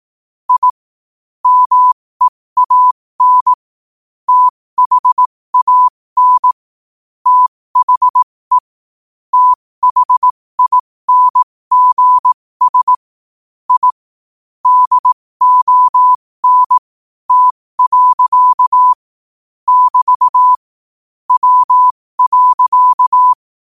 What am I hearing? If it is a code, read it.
Morse code, heard as a keyed tone, about 18 wpm, spaced as for 15 wpm, 1000 Hz. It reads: I MEAN THAN THE THINGS I DON T. <BT> W.
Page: 15 WPM morse code quotes